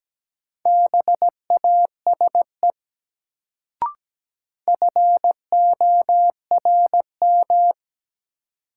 Morse code, heard as a keyed tone, about 17 wpm, 700 Hz. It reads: BASE FORM
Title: Sentences from Top 1000 Words - Review Part 1 - New Word Drill - 17wpm